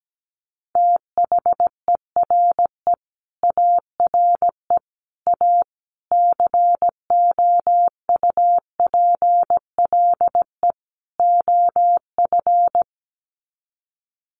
Morse code, heard as a keyed tone, about 17 wpm, 690 Hz. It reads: THERE ARE A COUPLE OF
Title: Sets of 5 Words Encore 17wpm